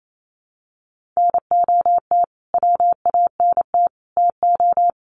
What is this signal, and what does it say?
Morse code, heard as a keyed tone, about 28 wpm, 700 Hz. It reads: NOT WANT TO
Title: Sets of 5 Words Encore2 28wpm